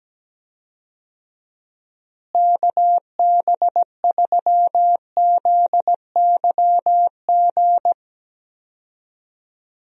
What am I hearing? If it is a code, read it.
Morse code, heard as a keyed tone, about 17 wpm, 700 Hz. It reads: KB3ZYG
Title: USA Call Signs - 6 Characters - Repeated 3x - 17wpm